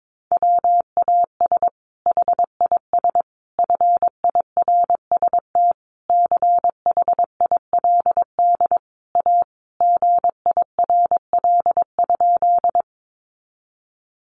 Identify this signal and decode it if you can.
Morse code, heard as a keyed tone, about 22 wpm, 700 Hz. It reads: WAS HIS FIRST CHILD A GIRL?
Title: Sentences from Top 800 Words 22wpm